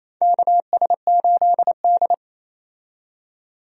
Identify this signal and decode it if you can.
Morse code, heard as a keyed tone, about 28 wpm, 700 Hz. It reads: KS8D
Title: USA Call Signs - 4 Characters - 28wpm